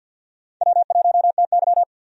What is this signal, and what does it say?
Morse code, heard as a keyed tone, about 53 wpm, 700 Hz. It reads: W1TY
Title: USA Call Signs - 4 Characters - 50wpm